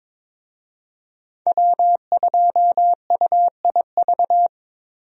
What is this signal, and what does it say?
Morse code, heard as a keyed tone, about 21 wpm, 700 Hz. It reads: W2UIV
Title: USA Call Signs - 5 Characters - Repeated 3x - 22wpm